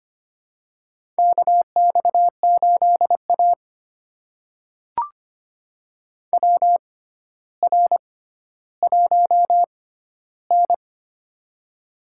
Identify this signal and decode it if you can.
Morse code, heard as a keyed tone, about 25 wpm, spaced as for 13 wpm, 700 Hz. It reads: KX8A WR1N
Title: USA Call Signs - 4 Characters - 6x Character Spacing - 25wpm